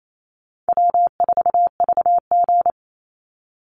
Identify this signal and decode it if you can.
Morse code, heard as a keyed tone, about 28 wpm, 700 Hz. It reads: W4VG
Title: Call Signs Repeated 3x - 28wpm